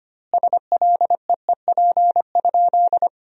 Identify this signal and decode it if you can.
Morse code, heard as a keyed tone, about 25 wpm, 700 Hz. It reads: SLEEP?
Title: 2x Word-Spacing - Sentences from Top 500 Words - 25wpm